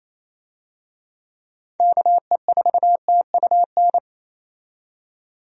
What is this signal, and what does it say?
Morse code, heard as a keyed tone, about 27 wpm, 700 Hz. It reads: KE4TUN
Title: USA Call Signs - 6 Characters - 28wpm